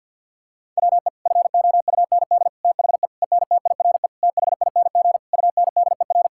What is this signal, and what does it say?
Morse code, heard as a keyed tone, about 50 wpm, 700 Hz. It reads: WE FOUND THE ENTIRE THING UNDER
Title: Sentences from Top 1000 Words - Review Part 5 - 50wpm